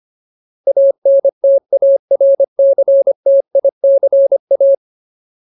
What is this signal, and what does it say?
Morse code, heard as a keyed tone, about 25 wpm, 550 Hz. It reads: ANTARCTICA
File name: Ultimate Word List / 25WPM Ultimate